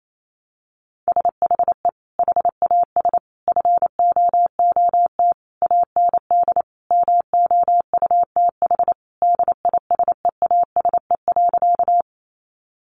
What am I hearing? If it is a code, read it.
Morse code, heard as a keyed tone, about 28 wpm, 700 Hz. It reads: SHE HAS FOOT AND MOUTH DISEASE.
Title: Sentences from Top 1500 Words 28wpm